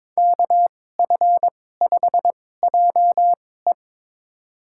Morse code, heard as a keyed tone, about 22 wpm, spaced as for 16 wpm, 700 Hz. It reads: KF5JE
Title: USA Call Signs - 5 Characters - 2x Character Spacing - 22wpm